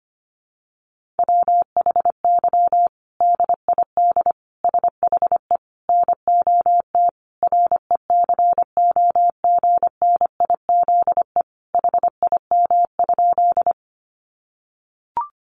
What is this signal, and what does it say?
Morse code, heard as a keyed tone, about 25 wpm, 700 Hz. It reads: WHY DID SHE NOT RECOGNIZE HIM?
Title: Sentences from Top 1400 Words 25wpm